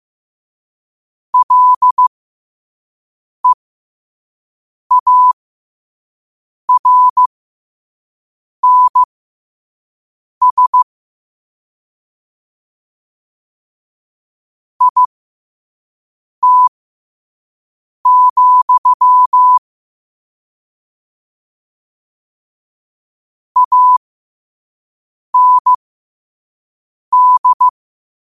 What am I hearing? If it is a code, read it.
Morse code, heard as a keyed tone, about 15 wpm, spaced as for 5 wpm, 1000 Hz. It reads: LEARNS IT, AND
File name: qotdfiles / 5wpm